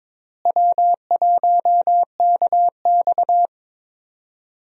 Morse code, heard as a keyed tone, about 22 wpm, 700 Hz. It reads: W1KX
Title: USA Call Signs - 4 Characters - Repeated 2x - 22wpm